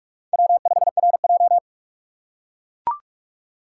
Morse code, heard as a keyed tone, about 44 wpm, 700 Hz. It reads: W5RJ